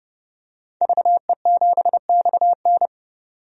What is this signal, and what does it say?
Morse code, heard as a keyed tone, about 29 wpm, 700 Hz. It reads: VE7XN
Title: DXCC-100 Call Signs - 5 Characters - 30wpm